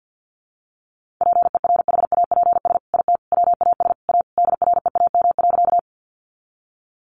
Morse code, heard as a keyed tone, about 48 wpm, 700 Hz. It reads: PERHAPS IT WAS A DREAM.
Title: Sentences from Top 1500 Words 50wpm